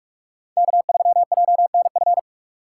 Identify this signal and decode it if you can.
Morse code, heard as a keyed tone, about 47 wpm, 700 Hz. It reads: K3JNF
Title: USA Call Signs - 4x Character Spacing - 45wpm